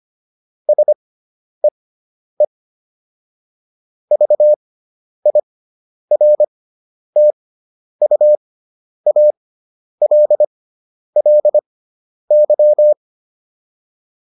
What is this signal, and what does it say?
Morse code, heard as a keyed tone, about 25 wpm, spaced as for 10 wpm, 600 Hz. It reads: SEE VIRTUALLY